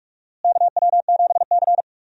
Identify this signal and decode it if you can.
Morse code, heard as a keyed tone, about 46 wpm, 700 Hz. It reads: KW7C
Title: USA Call Signs - 4 Characters - Repeated 2x - 45wpm